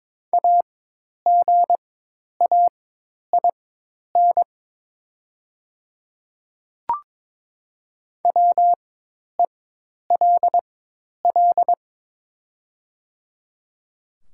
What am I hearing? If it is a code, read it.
Morse code, heard as a keyed tone, about 22 wpm, spaced as for 10 wpm, 690 Hz. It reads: AGAIN WELL
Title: 4x Character Spacing - Top 200 Words - 22wpm